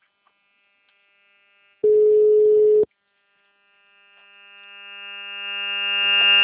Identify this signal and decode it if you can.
Ringback tone (Europe, 425 Hz).